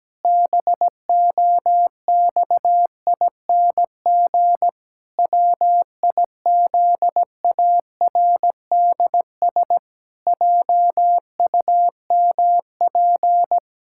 Morse code, heard as a keyed tone, about 17 wpm, 700 Hz. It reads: BOXING WIZARDS JUMP
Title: Pangram Challenge v2 - 17wpm